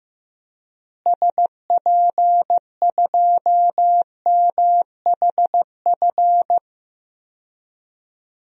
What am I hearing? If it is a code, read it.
Morse code, heard as a keyed tone, about 16 wpm, 700 Hz. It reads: SP2MHF
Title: DXCC-100 Call Signs - 6 Characters - Repeated 2x - 15wpm